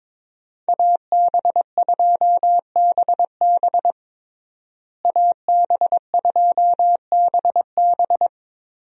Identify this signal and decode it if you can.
Morse code, heard as a keyed tone, about 21 wpm, 700 Hz. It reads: AB2BB AB2BB
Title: USA Call Signs - 5 Characters - Repeated 3x - 22wpm